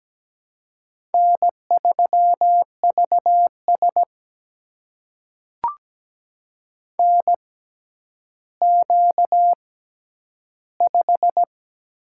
Morse code, heard as a keyed tone, about 17 wpm, spaced as for 10 wpm, 700 Hz. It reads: N3VS NQ5
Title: USA Call Signs - 4 Characters - 6x Character Spacing - 17wpm